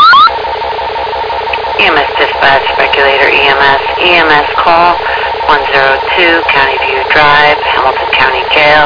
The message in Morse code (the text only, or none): none